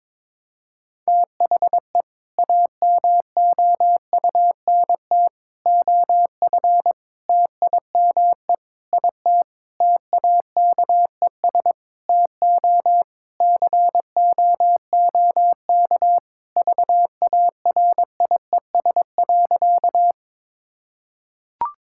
Morse code, heard as a keyed tone, about 22 wpm, 700 Hz. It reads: THE AMOUNT OF TIME IT TAKES TO COOK VARIES.
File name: Sentences with 7-Word Phrases - 22wpm